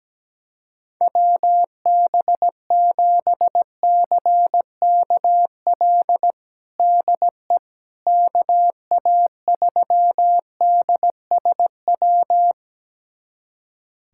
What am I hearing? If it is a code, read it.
Morse code, heard as a keyed tone, about 17 wpm, 700 Hz. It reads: WB7CKL DE KA3DSW
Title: USA Callsign DE Callsign - 6 Characters - 17wpm